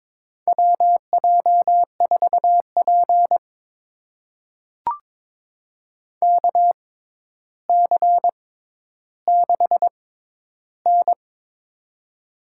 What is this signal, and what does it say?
Morse code, heard as a keyed tone, about 22 wpm, spaced as for 11 wpm, 700 Hz. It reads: WJ4P KC6N
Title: USA Call Signs - 4 Characters - 6x Character Spacing - 22wpm